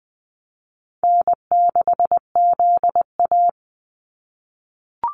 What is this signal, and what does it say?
Morse code, heard as a keyed tone, about 20 wpm, 700 Hz. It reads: N6ZA
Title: NA SOTA Activator Callsigns 20wpm